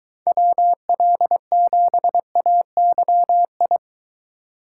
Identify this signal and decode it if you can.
Morse code, heard as a keyed tone, about 23 wpm, 700 Hz. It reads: WL7AYI